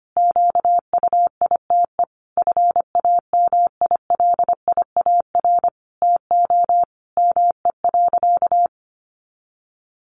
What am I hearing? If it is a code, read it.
Morse code, heard as a keyed tone, about 25 wpm, 700 Hz. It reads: QUITE FAMILIAR TO ME.
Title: Sentences from Top 1100 Words 25wpm